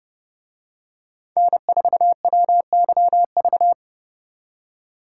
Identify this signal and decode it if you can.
Morse code, heard as a keyed tone, about 31 wpm, 700 Hz. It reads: N4WYV